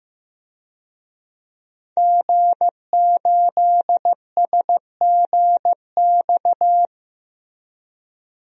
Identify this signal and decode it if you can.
Morse code, heard as a keyed tone, about 15 wpm, 700 Hz. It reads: G8SGX
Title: DXCC-100 Call Signs - 5 Characters - Repeated 3x - 15wpm